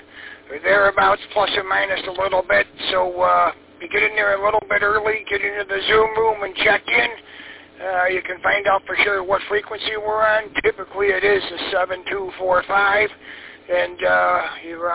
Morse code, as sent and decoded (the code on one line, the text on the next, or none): none